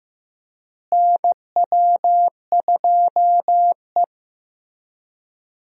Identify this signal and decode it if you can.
Morse code, heard as a keyed tone, about 15 wpm, 700 Hz. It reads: NW2E
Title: USA Call Signs - 4 Characters - Repeated 3x - 15wpm